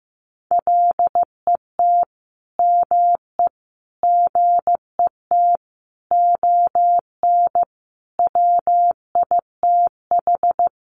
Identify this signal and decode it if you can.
Morse code, heard as a keyed tone, about 15 wpm, 700 Hz. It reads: LET ME GET ON WITH
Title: Sentences from Top 300 Words 15wpm